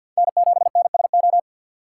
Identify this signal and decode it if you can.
Morse code, heard as a keyed tone, about 52 wpm, 700 Hz. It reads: N7NSO
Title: USA Call Signs - 5 Characters - 2x Character Spacing - 50wpm